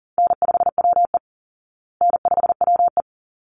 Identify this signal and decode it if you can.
Morse code, heard as a keyed tone, about 40 wpm, 700 Hz. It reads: N5WE N5WE
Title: Easy Call Signs Repeated 2x - 40wpm